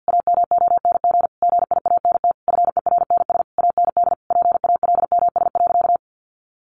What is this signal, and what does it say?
Morse code, heard as a keyed tone, about 50 wpm, 700 Hz. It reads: AMONG GIANT FERNS AND PALMS.
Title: A Sound Of Thunder 50wpm